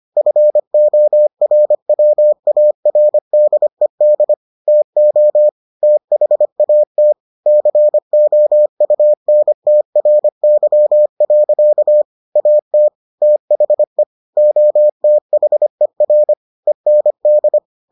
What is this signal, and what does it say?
Morse code, heard as a keyed tone, about 25 wpm, 600 Hz. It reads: FORWARDED TO THAT COUNTRY. AT THE OTHER END